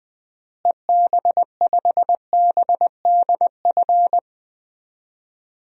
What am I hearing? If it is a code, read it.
Morse code, heard as a keyed tone, about 20 wpm, 700 Hz. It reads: EB5BDF
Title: DXCC-100 Call Signs - 6 Characters - Repeated 3x - 20wpm